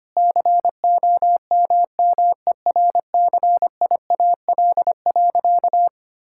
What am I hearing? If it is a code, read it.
Morse code, heard as a keyed tone, about 25 wpm, 700 Hz. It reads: COMMERCIAL.
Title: Sentences from Top 1000 Words - Review Part 5 - 25wpm